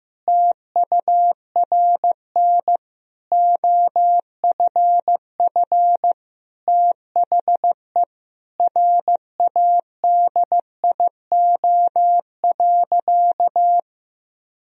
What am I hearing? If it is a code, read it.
Morse code, heard as a keyed tone, about 16 wpm, 700 Hz. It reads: TURN OFF THE RADIO.
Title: Sentences from Top 1000 Words - Review Part 3 - 15wpm